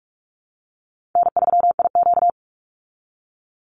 Morse code, heard as a keyed tone, about 44 wpm, 700 Hz. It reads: N3IQ
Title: Call Signs Repeated 3x - 45wpm